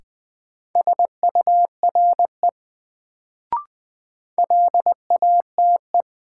Morse code, heard as a keyed tone, about 21 wpm, 700 Hz. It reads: SURE LATE
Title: Sentences from Top 300 Words - New Word Drill - 20wpm